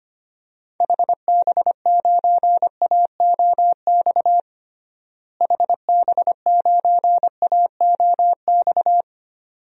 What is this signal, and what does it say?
Morse code, heard as a keyed tone, about 25 wpm, 700 Hz. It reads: HB9AOX HB9AOX
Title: DXCC-100 Call Signs - 6 Characters - Repeated 2x - 25wpm